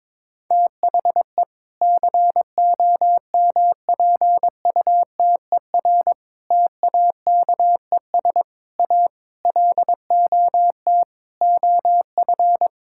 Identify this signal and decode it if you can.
Morse code, heard as a keyed tone, about 21 wpm, 700 Hz. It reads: THE COMPUTER TAKES A LOT OF
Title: Sentences from Top 1000 Words - Review Part 6 - 22wpm